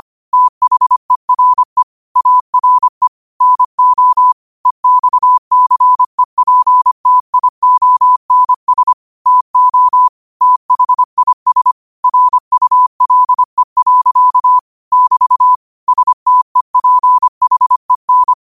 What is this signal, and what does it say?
Morse code, heard as a keyed tone, about 25 wpm, 1000 Hz. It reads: THERE ARE NO EXCEPTIONS TO THIS RULE. <BT> STEPHEN
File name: qotdfiles / 25wpm